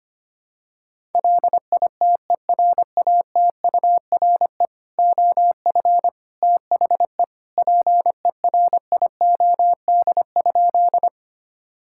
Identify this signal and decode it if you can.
Morse code, heard as a keyed tone, about 25 wpm, 700 Hz. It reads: LITERATURE OF THE PERIOD?
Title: Sentences from Top 1000 Words - Review Part 5 - 25wpm